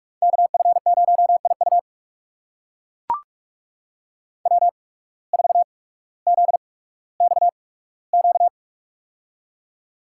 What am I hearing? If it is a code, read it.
Morse code, heard as a keyed tone, about 43 wpm, spaced as for 19 wpm, 700 Hz. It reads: KF0IU W4ZXQ
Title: USA Call Signs - 5 Characters - 8x Character Spacing - 45wpm